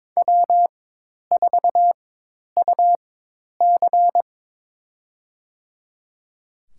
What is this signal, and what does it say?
Morse code, heard as a keyed tone, about 22 wpm, spaced as for 10 wpm, 700 Hz. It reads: W4UC
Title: USA Call Signs - 4x Character Spacing - 22wpm